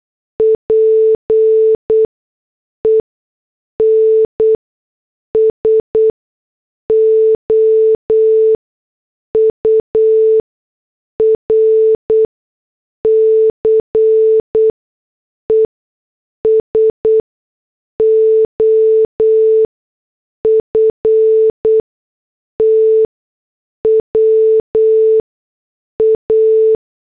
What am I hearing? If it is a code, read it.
Morse code, heard as a keyed tone, about 8 wpm, spaced as for 6 wpm, 440 Hz. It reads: PEN SOURCE SOFTWA
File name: morse / headlines